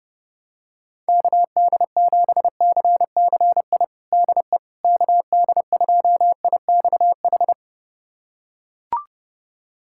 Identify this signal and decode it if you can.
Morse code, heard as a keyed tone, about 32 wpm, 700 Hz. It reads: KD7CCI DE KD2IXH
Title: USA Callsign DE Callsign - 6 Characters - 30wpm